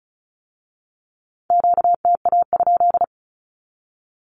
Morse code, heard as a keyed tone, about 36 wpm, 700 Hz. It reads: QTA?
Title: Q-Codes with Spoken Meaning - 35wpm